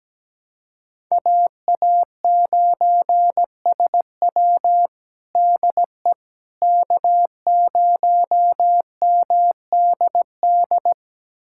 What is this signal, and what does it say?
Morse code, heard as a keyed tone, about 17 wpm, 700 Hz. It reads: AA9SW DE K0MDD
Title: USA Callsign DE Callsign 17wpm